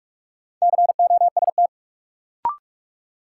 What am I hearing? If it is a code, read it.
Morse code, heard as a keyed tone, about 46 wpm, 700 Hz. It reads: COST
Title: Sentences from Top 800 Words - New Word Drill - 45wpm